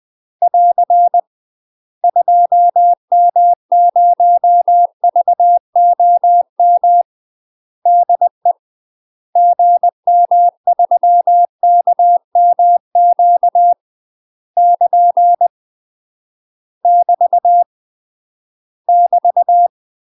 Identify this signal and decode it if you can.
Morse code, heard as a keyed tone, about 20 wpm, 700 Hz. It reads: <AR> 2M0VOM DE GM3KMQ <KN> <BT> <BT>
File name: UK MASTER QSOs TRAINING / 20WPM / UK MASTER QSOs 0015